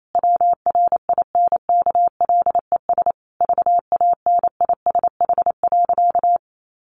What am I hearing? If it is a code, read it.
Morse code, heard as a keyed tone, about 28 wpm, 700 Hz. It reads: WRINKLES VANISH.
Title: A Sound Of Thunder 28wpm